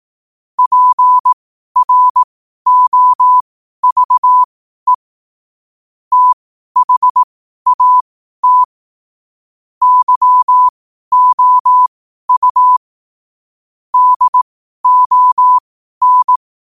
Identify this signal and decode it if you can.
Morse code, heard as a keyed tone, about 18 wpm, spaced as for 12 wpm, 1000 Hz. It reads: PROVE THAT YOU DON
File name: qotdfiles / 12wpm